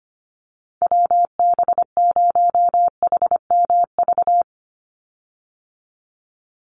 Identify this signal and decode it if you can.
Morse code, heard as a keyed tone, about 25 wpm, 700 Hz. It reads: WB0HMV
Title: USA Call Signs 25wpm